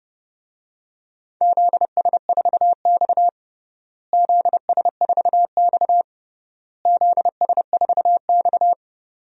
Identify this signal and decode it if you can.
Morse code, heard as a keyed tone, about 30 wpm, 700 Hz. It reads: ZS4X ZS4X ZS4X
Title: DXCC-100 Call Signs - 4 Characters - Repeated 3x - 30wpm